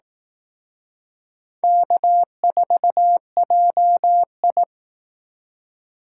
Morse code, heard as a keyed tone, about 19 wpm, 700 Hz. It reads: K4JI
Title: Speed Racing - USA 1x2 Call Signs 15wpm